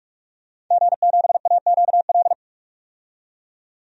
Morse code, heard as a keyed tone, about 45 wpm, 700 Hz. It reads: G7AQL
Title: DXCC-100 Call Signs - 5 Characters - 45wpm